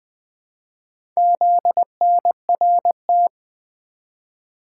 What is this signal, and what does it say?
Morse code, heard as a keyed tone, about 21 wpm, 700 Hz. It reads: ZNRT
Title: Four Letter-Number Combinations - 20wpm